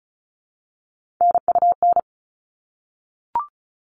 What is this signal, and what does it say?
Morse code, heard as a keyed tone, about 34 wpm, 700 Hz. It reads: NUN
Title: All Cut Numbers - 3-Digit 35wpm